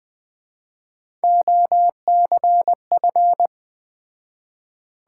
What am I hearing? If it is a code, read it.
Morse code, heard as a keyed tone, about 20 wpm, 700 Hz. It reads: OCF